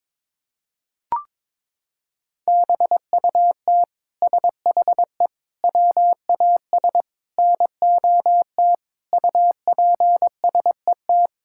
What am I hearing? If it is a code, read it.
Morse code, heard as a keyed tone, about 22 wpm, 700 Hz. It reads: BUT SHE WAS NOT UPSET
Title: Sentences from Top 1000 Words - Review Part 5 - 22wpm